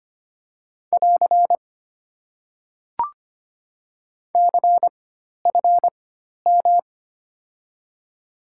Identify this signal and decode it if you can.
Morse code, heard as a keyed tone, about 25 wpm, spaced as for 12 wpm, 690 Hz. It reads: <AR> CFM
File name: Top 90 QSO Elements - 4x Character Spacing - 25wpm